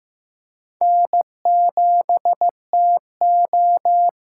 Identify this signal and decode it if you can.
Morse code, heard as a keyed tone, about 15 wpm, 700 Hz. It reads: N7TO
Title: USA Call Signs - 5 Characters - 6x Character Spacing - 15wpm